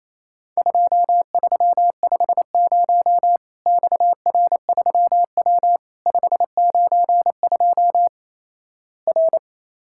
Morse code, heard as a keyed tone, about 28 wpm, 700 Hz. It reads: 2350 XR3W 592 R
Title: Worked All Europe DX Contest - QTC Exchange - 28wpm